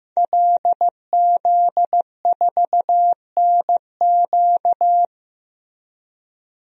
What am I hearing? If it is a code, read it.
Morse code, heard as a keyed tone, about 16 wpm, 700 Hz. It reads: LZ4NQ
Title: DXCC-100 Call Signs - 5 Characters - Repeated 3x - 15wpm